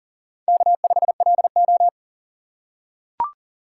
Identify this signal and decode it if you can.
Morse code, heard as a keyed tone, about 40 wpm, 700 Hz. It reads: K5LO
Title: USA Call Signs - 3x Character Spacing - 40wpm